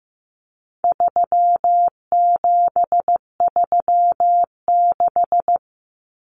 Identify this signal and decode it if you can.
Morse code, heard as a keyed tone, about 15 wpm, 700 Hz. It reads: 3736
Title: Numbers - 4-Digit 15wpm